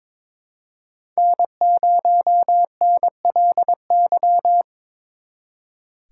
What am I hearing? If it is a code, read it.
Morse code, heard as a keyed tone, about 27 wpm, 700 Hz. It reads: N0NLY
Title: Speed Racing - USA 1x3 Call Signs 22wpm